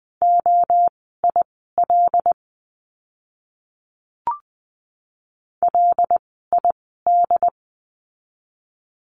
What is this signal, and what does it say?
Morse code, heard as a keyed tone, about 20 wpm, spaced as for 14 wpm, 700 Hz. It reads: OIL LID
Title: 075 Lesson 13f - Letter L - Words 20wpm